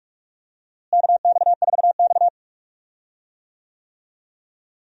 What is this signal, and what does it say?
Morse code, heard as a keyed tone, about 43 wpm, 700 Hz. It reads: KX4X
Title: USA Call Signs - 4 Characters - Repeated 3x - 45wpm